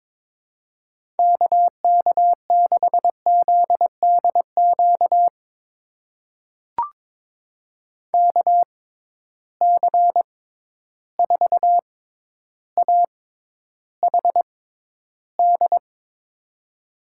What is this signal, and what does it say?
Morse code, heard as a keyed tone, about 22 wpm, spaced as for 11 wpm, 700 Hz. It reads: KK6ZDQ KC4AHD
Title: USA Call Signs - 6 Characters - 6x Character Spacing - 22wpm